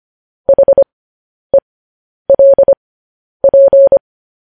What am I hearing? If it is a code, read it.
Morse code, heard as a keyed tone, about 25 wpm, spaced as for 10 wpm, 550 Hz. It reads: HELP
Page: news headlines in morse code at 10 wpm